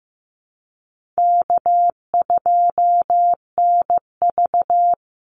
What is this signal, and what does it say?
Morse code, heard as a keyed tone, about 15 wpm, 700 Hz. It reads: K2NV
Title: Easy Call Signs Repeated 2x - 15wpm